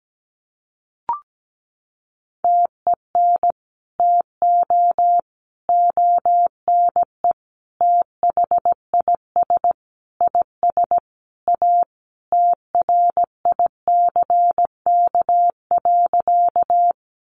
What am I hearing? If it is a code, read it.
Morse code, heard as a keyed tone, about 17 wpm, 700 Hz. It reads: TEN TO ONE THIS IS A TRICK.
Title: Sentences from Top 1500 Words 17wpm